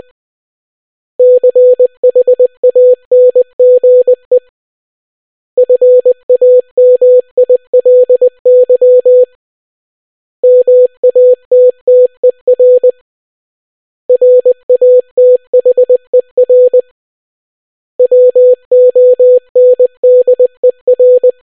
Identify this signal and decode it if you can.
Morse code, heard as a keyed tone, about 20 wpm, 500 Hz. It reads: CHANGE FAMILY MATTER RATHER WONDER